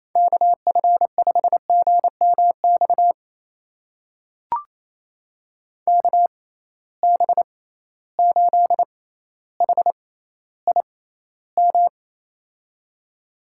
Morse code, heard as a keyed tone, about 28 wpm, spaced as for 14 wpm, 700 Hz. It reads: KF5GMX KB8HIM
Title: USA Call Signs - 6 Characters - 6x Character Spacing - 28wpm